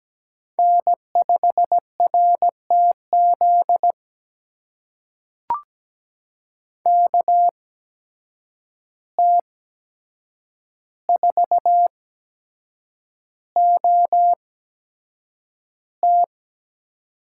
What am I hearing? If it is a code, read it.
Morse code, heard as a keyed tone, about 17 wpm, spaced as for 7 wpm, 700 Hz. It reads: N5RTZ KT4OT
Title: USA Call Signs - 5 Characters - 8x Character Spacing - 17wpm